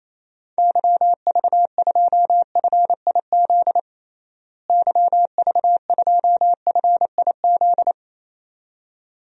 Morse code, heard as a keyed tone, about 28 wpm, 700 Hz. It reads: YV2FIZ YV2FIZ
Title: DXCC-100 Call Signs - 6 Characters - Repeated 2x - 28wpm